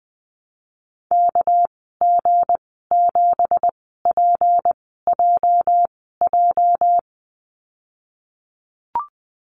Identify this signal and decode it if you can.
Morse code, heard as a keyed tone, about 20 wpm, spaced as for 15 wpm, 700 Hz. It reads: KG7PJJ
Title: 207 Lesson 33g - Letter j - Call Signs 20wpm